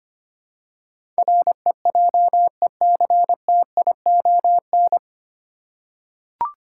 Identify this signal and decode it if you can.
Morse code, heard as a keyed tone, about 25 wpm, 700 Hz. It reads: REJECTION
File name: practice / 25wpm / 9 Letter Words - First 3 Letters Given - 25wpm